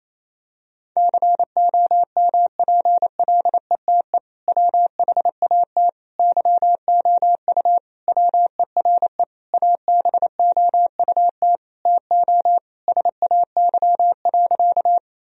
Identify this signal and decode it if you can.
Morse code, heard as a keyed tone, about 28 wpm, 700 Hz. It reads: COMPLETE WHAT YOU WERE ABOUT TO SAY.
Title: Sentences from Top 1000 Words - Review Part 2 - 28wpm